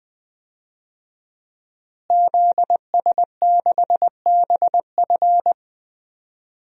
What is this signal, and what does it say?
Morse code, heard as a keyed tone, about 20 wpm, 700 Hz. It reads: ZS6BF
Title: DXCC-100 Call Signs - 5 Characters - Repeated 2x - 20wpm